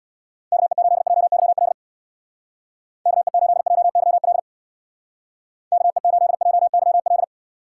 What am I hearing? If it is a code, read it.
Morse code, heard as a keyed tone, about 61 wpm, 700 Hz. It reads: KE8JYL KE8JYL KE8JYL
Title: Speed Racing - USA Call Signs 50wpm